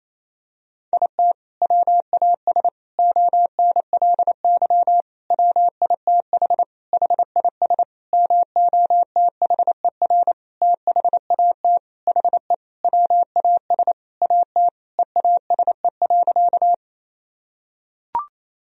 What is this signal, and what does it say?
Morse code, heard as a keyed tone, about 28 wpm, 700 Hz. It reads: IT WAS ONLY WITH HIS MOTHER THAT HE WAS AT EASE.
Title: Sentences from Top 1000 Words - Review Part 1 - 28wpm